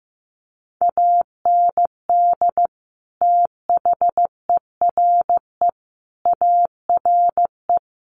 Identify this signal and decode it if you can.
Morse code, heard as a keyed tone, about 15 wpm, 700 Hz. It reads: AND THERE ARE
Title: Sets of 5 Words Encore2 15wpm